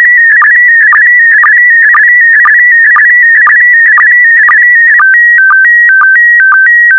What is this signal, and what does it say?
SSTV (slow-scan television), PD 120.
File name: SSTV PD120 Leo